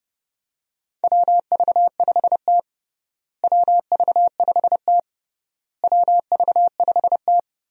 Morse code, heard as a keyed tone, about 30 wpm, 700 Hz. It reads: WV5T WV5T WV5T
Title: USA Call Signs - 4 Characters - Repeated 3x - 30wpm